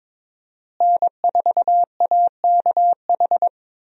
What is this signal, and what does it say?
Morse code, heard as a keyed tone, about 22 wpm, 700 Hz. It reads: N4AKH
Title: USA Call Signs - 5 Characters - Repeated 3x - 22wpm